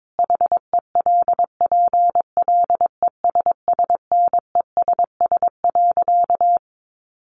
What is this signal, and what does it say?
Morse code, heard as a keyed tone, about 22 wpm, 700 Hz. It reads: HELPLESSNESS.
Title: A Sound Of Thunder 22wpm